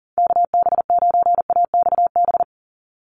Morse code, heard as a keyed tone, about 40 wpm, 690 Hz. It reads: KB9AXB
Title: USA Call Signs 40wpm